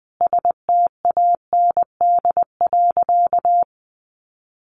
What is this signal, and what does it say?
Morse code, heard as a keyed tone, about 20 wpm, 700 Hz. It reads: STAND.
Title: Sentences from Top 1300 Words 20wpm